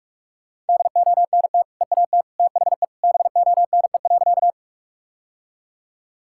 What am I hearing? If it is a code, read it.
Morse code, heard as a keyed tone, about 46 wpm, 700 Hz. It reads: DONT EAT THE BONE.
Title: Sentences from Top 1600 Words 45wpm